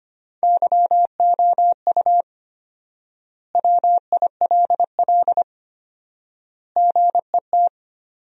Morse code, heard as a keyed tone, about 25 wpm, 700 Hz. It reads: YOU WILL GET
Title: 4x Word-Spacing - Sentences from Top 200 Words - 25wpm